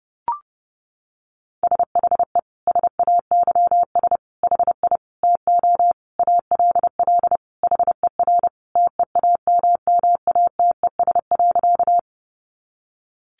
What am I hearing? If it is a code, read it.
Morse code, heard as a keyed tone, about 30 wpm, 700 Hz. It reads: SHE SAYS HI TO ALL HER TEAMMATES.
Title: Easy Phrases 30wpm